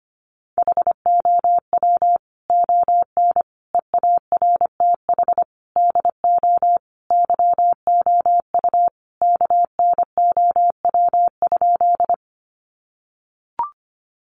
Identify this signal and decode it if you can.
Morse code, heard as a keyed tone, about 25 wpm, 700 Hz. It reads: HOW ON EARTH DO YOU KNOW?
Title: Sentences from Top 200 Words 25wpm